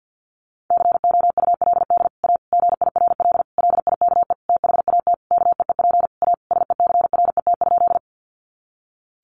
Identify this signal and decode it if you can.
Morse code, heard as a keyed tone, about 50 wpm, 700 Hz. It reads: COULD A GIRL LIKE THAT KEEP A SECRET?
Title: Sentences from Top 1200 Words 50wpm